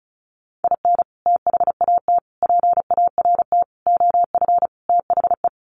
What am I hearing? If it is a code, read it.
Morse code, heard as a keyed tone, about 34 wpm, 700 Hz. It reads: IN THAT PART OF THE
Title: Sets of 5 Words Encore2 35wpm